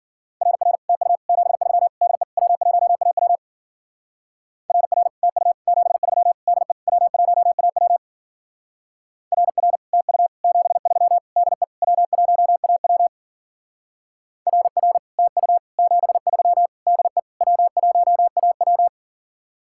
Morse code, heard as a keyed tone, about 50 wpm, 700 Hz. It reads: RR TU 73 DE W1AW RR TU 73 DE W1AW RR TU 73 DE W1AW RR TU 73 DE W1AW